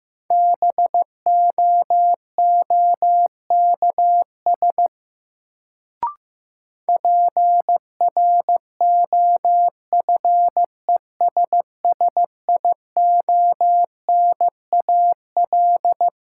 Morse code, heard as a keyed tone, about 15 wpm, 700 Hz. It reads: BOOKS PROFESSIONAL
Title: Sentences from Top 1000 Words - Review Part 4 - New Word Drill - 15wpm